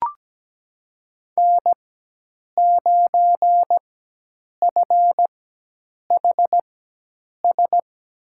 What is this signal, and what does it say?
Morse code, heard as a keyed tone, about 17 wpm, spaced as for 8 wpm, 700 Hz. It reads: N9FHS
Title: USA Call Signs - 5 Characters - 4x Character Spacing - 17wpm